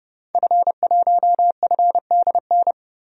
Morse code, heard as a keyed tone, about 30 wpm, 700 Hz. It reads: F1FDN
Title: DXCC-100 Call Signs - 5 Characters - 30wpm